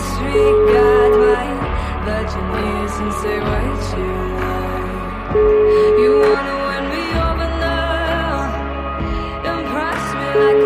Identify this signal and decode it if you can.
Ringback tone (Europe, 425 Hz), with noise.